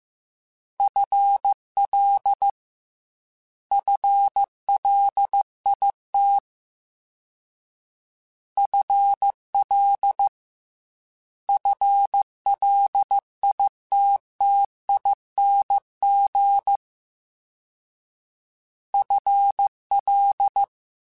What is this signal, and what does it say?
Morse code, heard as a keyed tone, about 15 wpm, 800 Hz. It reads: FL FLIT FL FLITTING FL
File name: FL / FL mp3s / FL Prefixes 1x1 @13